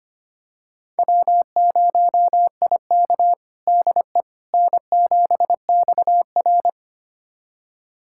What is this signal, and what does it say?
Morse code, heard as a keyed tone, about 25 wpm, 700 Hz. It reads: W0IK DE N7XR
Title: USA Callsign DE Callsign - 4 Characters - 25wpm